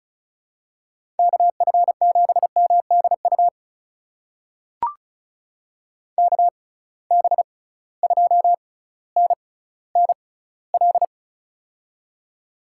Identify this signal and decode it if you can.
Morse code, heard as a keyed tone, about 36 wpm, spaced as for 18 wpm, 700 Hz. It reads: KF7MDU KB2NNL
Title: USA Call Signs - 6 Characters - 6x Character Spacing - 35wpm